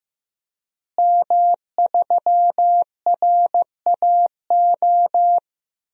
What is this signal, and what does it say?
Morse code, heard as a keyed tone, about 15 wpm, 700 Hz. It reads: M3RAO
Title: DXCC-100 Call Signs - 5 Characters - Repeated 3x - 15wpm